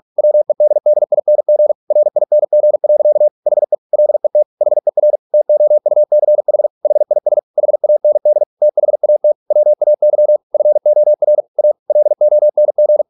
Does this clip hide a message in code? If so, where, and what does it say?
Yes: Morse code from 0.2 to 13.0 s, heard as a keyed tone, about 46 wpm, 600 Hz. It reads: WEDDING RING. HE LET HER TOUCH HIS HAND THAT WAY FOR A LONG